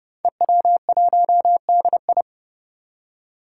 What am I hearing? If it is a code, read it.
Morse code, heard as a keyed tone, about 30 wpm, 700 Hz. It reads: EW1DI
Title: Five Challenging Letter-Number Combinations 30wpm